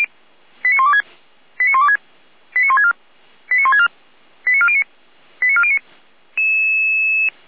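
Five-tone selective call (ZVEI), heard as quick runs of five tones, tones of about 70 ms. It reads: 89126 89126 89164 89164 89409 89409
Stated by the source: ZVEI five-tone call 89338, received by SDR.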